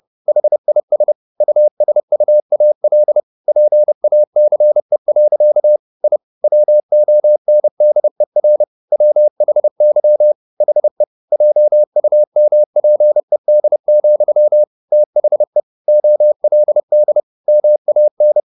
Morse code, heard as a keyed tone, about 30 wpm, 600 Hz. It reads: HIS USUAL PACE. I WONDER WHY HE JUMPED, THE OLD MAN
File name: the old man and the sea / the old man and the sea 1 30wpm